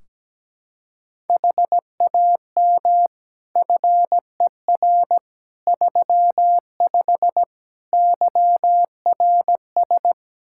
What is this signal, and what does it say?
Morse code, heard as a keyed tone, about 17 wpm, 700 Hz. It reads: HAM FER 35 YRS
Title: Practice QSO - Exchange 3 - More Info - 17wpm